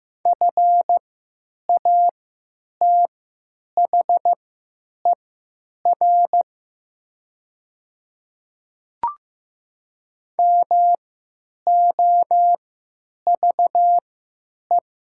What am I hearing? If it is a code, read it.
Morse code, heard as a keyed tone, about 15 wpm, spaced as for 9 wpm, 700 Hz. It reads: FATHER MOVE
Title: 3x Character Spacing - Top 200 Words - 15wpm